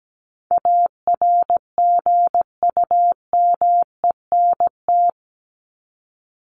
Morse code, heard as a keyed tone, about 17 wpm, 700 Hz. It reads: ARGUMENT
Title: Top 1251-1500 Words 17wpm